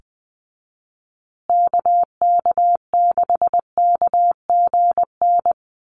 Morse code, heard as a keyed tone, about 20 wpm, 700 Hz. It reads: KK6KGN
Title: USA Call Signs Repeated 3x 20wpm